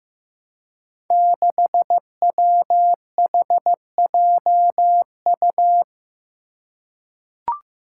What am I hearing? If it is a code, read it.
Morse code, heard as a keyed tone, about 15 wpm, 700 Hz. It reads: 6WHJU
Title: Five Challenging Letter-Number Combinations 15wpm